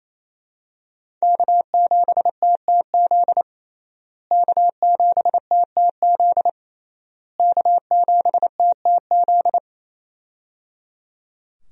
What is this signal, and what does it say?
Morse code, heard as a keyed tone, about 28 wpm, 700 Hz. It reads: K7TTZ K7TTZ K7TTZ
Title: USA Call Signs - 5 Characters - Repeated 3x - 28wpm